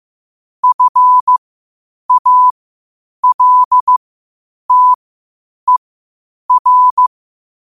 Morse code, heard as a keyed tone, about 15 wpm, spaced as for 8 wpm, 1000 Hz. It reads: FALTER